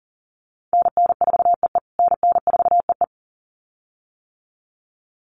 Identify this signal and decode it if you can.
Morse code, heard as a keyed tone, about 40 wpm, 700 Hz. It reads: NN4EE NN4EE
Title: USA Call Signs Repeated 2x 40wpm